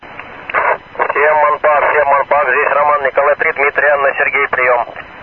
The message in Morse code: none

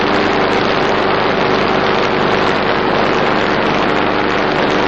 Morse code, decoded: none

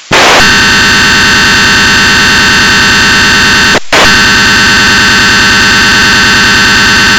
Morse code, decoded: none

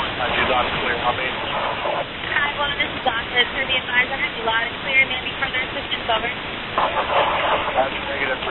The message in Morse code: none

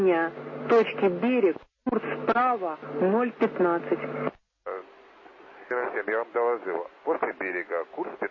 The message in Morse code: none

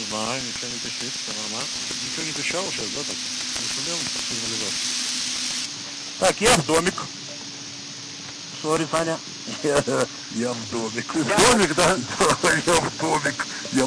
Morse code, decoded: none

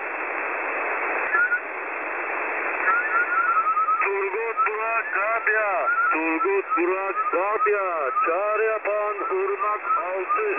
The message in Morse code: none

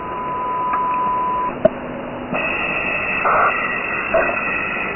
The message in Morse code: none

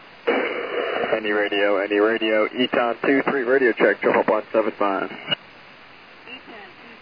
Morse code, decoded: none

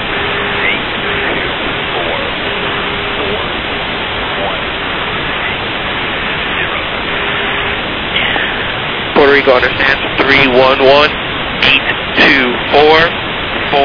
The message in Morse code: none

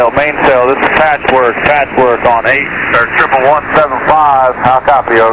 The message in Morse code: none

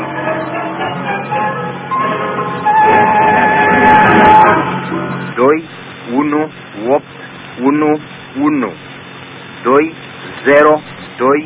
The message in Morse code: none

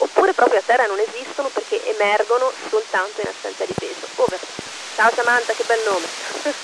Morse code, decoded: none

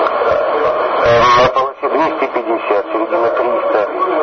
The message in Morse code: none

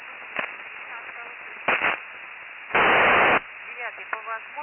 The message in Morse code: none

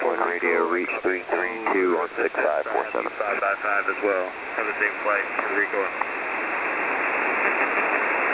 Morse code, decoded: none

